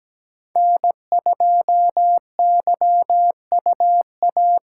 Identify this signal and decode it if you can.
Morse code, heard as a keyed tone, about 17 wpm, 700 Hz. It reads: N2YUA
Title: USA Call Signs - 2x Character Spacing - 17wpm